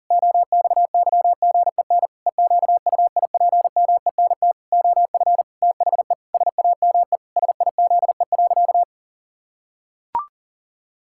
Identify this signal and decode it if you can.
Morse code, heard as a keyed tone, about 40 wpm, 700 Hz. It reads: OXYGEN EQUIPMENT OF THE SAME SIZE.
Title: Pangram Challenge v2 - 40wpm